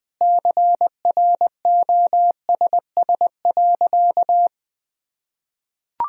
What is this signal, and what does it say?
Morse code, heard as a keyed tone, about 21 wpm, 700 Hz. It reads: CROSS.
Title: Sentences from Top 1000 Words - Review Part 1 - 20wpm